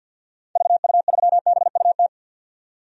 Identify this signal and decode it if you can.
Morse code, heard as a keyed tone, about 52 wpm, 700 Hz. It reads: VU3BUT
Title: DXCC-100 Call Signs - 6 Characters - 50wpm